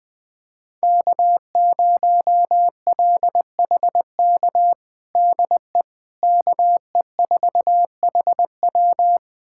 Morse code, heard as a keyed tone, about 21 wpm, 700 Hz. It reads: K0LHK DE KE4HW
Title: USA Callsign DE Callsign - 5 Characters - 20wpm